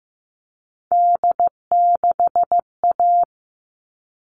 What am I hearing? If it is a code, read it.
Morse code, heard as a keyed tone, about 15 wpm, 700 Hz. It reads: D6A
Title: All Cut Numbers - 3-Digit 15wpm